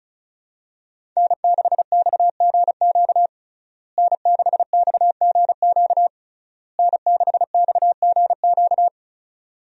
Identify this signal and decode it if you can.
Morse code, heard as a keyed tone, about 36 wpm, 700 Hz. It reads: N6XGQ N6XGQ N6XGQ
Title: USA Call Signs - 5 Characters - Repeated 3x - 35wpm